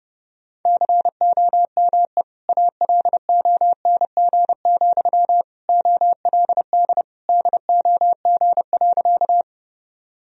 Morse code, heard as a keyed tone, about 30 wpm, 700 Hz. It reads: COME ALONG, OLD DOG.
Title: Sentences from Top 1000 Words - Review Part 2 - 30wpm